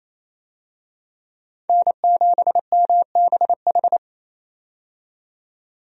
Morse code, heard as a keyed tone, about 28 wpm, 700 Hz. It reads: N7MBH
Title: USA Call Signs - 5 Characters - 28wpm